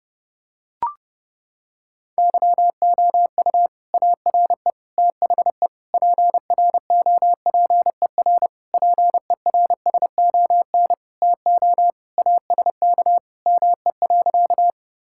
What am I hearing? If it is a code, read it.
Morse code, heard as a keyed tone, about 32 wpm, 700 Hz. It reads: YOU ARE THE PROPER PERSON TO ASK ME.
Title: Sentences from Top 1000 Words - Review Part 5 - 30wpm